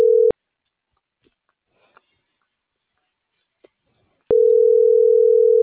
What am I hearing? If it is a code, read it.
Ringback tone (North America), 440 + 480 Hz.